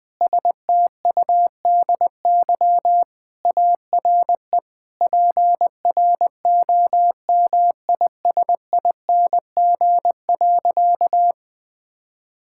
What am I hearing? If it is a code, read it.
Morse code, heard as a keyed tone, about 20 wpm, 700 Hz. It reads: STUDY ARE PROMISING.
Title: Sentences with 5-Word Phrases - 20wpm